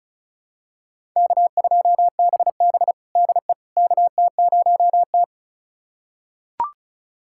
Morse code, heard as a keyed tone, about 36 wpm, 700 Hz. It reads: K2BB DE KT0T
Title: USA Callsign DE Callsign - 4 Characters - 35wpm